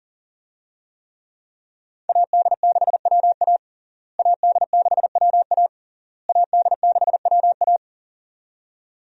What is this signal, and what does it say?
Morse code, heard as a keyed tone, about 40 wpm, 700 Hz. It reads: AD6WA AD6WA AD6WA
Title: USA Call Signs - 5 Characters - Repeated 3x - 40wpm